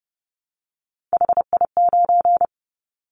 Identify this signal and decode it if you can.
Morse code, heard as a keyed tone, about 30 wpm, 700 Hz. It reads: HI9
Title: Callsign Prefixes for Top 100 DXCC - 30wpm